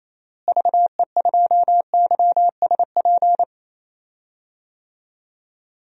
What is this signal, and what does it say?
Morse code, heard as a keyed tone, about 27 wpm, 700 Hz. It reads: VE2YSP
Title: DXCC-100 Call Signs - 6 Characters - Repeated 2x - 28wpm